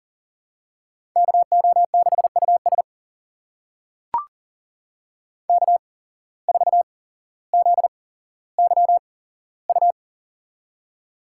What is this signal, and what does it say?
Morse code, heard as a keyed tone, about 40 wpm, spaced as for 17 wpm, 700 Hz. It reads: KO6US K4ZYU
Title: USA Call Signs - 5 Characters - 8x Character Spacing - 40wpm